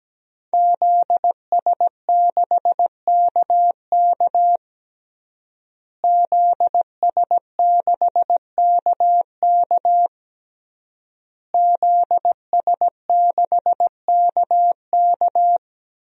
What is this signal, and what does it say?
Morse code, heard as a keyed tone, about 17 wpm, 700 Hz. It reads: ZS6KK ZS6KK ZS6KK
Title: DXCC-100 Call Signs - 5 Characters - Repeated 3x - 17wpm